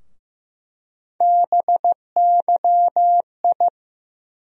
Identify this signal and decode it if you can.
Morse code, heard as a keyed tone, about 15 wpm, 700 Hz. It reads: BYI
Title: USA Call Sign Postfixes - 3 Letters 15wpm